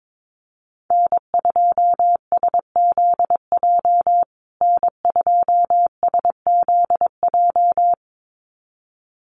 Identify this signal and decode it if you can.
Morse code, heard as a keyed tone, about 22 wpm, 700 Hz. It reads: N2SZJ N2SZJ
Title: USA Call Signs Repeated 2x 22wpm